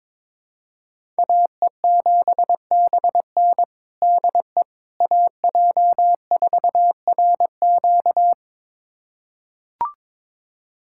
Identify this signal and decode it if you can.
Morse code, heard as a keyed tone, about 22 wpm, 700 Hz. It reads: AE7BN DE AJ4RQ
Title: USA Callsign DE Callsign - 5 Characters - 22wpm